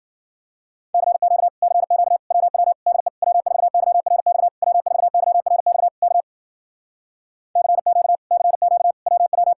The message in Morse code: CQ CQ WW DE W4YRX W4YRX K CQ CQ WW

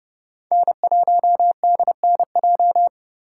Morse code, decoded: N1DNJ

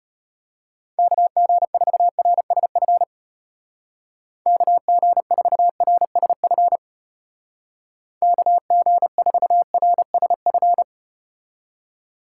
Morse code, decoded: KG4RSF KG4RSF KG4RSF

